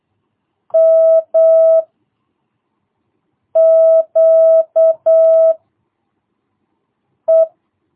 Morse code, heard as a keyed tone, about 8 wpm, 660 Hz. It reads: M Q E
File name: jh8cbh / 2022cw